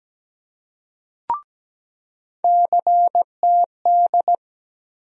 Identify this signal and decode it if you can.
Morse code, heard as a keyed tone, about 17 wpm, 700 Hz. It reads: CTD
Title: USA Call Sign Postfixes - 3 Letters 17wpm